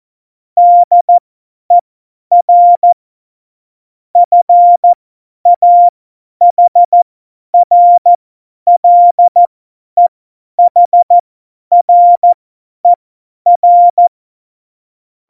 Morse code, heard as a keyed tone, about 14 wpm, spaced as for 10 wpm, 700 Hz. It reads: DER FAHRLEHRER
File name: Morsekurs / Witze / Witz 08. Januar 2019, 50 F70